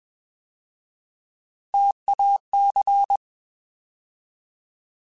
Morse code, heard as a keyed tone, about 21 wpm, 800 Hz. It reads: TAC
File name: Alphabetical / Alphabetical mp3s / Ts @20